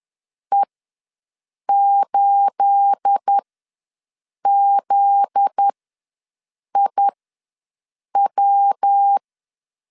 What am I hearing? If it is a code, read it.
Morse code, heard as a keyed tone, about 10 wpm, spaced as for 6 wpm, 800 Hz. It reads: E8ZIW
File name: LEZIONI / 9 28 ws 10wpm 4124